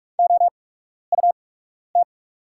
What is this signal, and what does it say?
Morse code, heard as a keyed tone, about 48 wpm, spaced as for 12 wpm, 690 Hz. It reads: OUT
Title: 8x Character Spacing - Top 500 Words - 45wpm